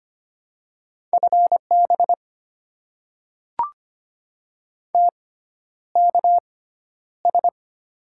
Morse code, heard as a keyed tone, about 25 wpm, spaced as for 11 wpm, 700 Hz. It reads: FB TKS
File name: Top 90 QSO Elements - 6x Character Spacing - 25wpm